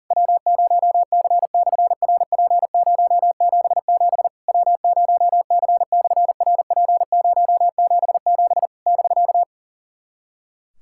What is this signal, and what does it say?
Morse code, heard as a keyed tone, about 44 wpm, 700 Hz. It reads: W0C/RP077 W0C/RP077 <BK>
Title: SOTA QSO Practice 40wpm